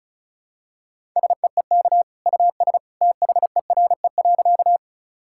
Morse code, heard as a keyed tone, about 36 wpm, 700 Hz. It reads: SEEK US THERE.
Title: Sentences from Top 1600 Words 35wpm